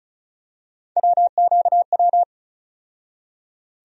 Morse code, heard as a keyed tone, about 34 wpm, 700 Hz. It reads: WQW
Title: USA Call Sign Postfixes - 3 Letters 35wpm